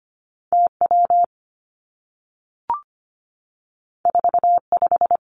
Morse code, heard as a keyed tone, about 25 wpm, 700 Hz. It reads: T W 4 5
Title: Two Challenging Letter-Number Combinations 25wpm